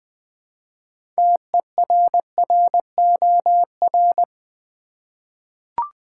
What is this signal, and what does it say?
Morse code, heard as a keyed tone, about 20 wpm, 700 Hz. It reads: TERROR
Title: 6 Letter Words - First 3 Letters Given - 20wpm